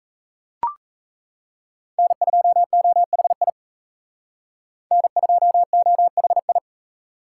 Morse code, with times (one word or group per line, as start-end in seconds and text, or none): 2.0-3.5 N2OHI
4.9-6.6 N2OHI